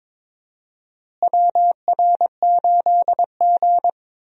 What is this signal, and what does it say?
Morse code, heard as a keyed tone, about 21 wpm, 700 Hz. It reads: WR8G
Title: USA Call Signs - 4 Characters - Repeated 2x - 22wpm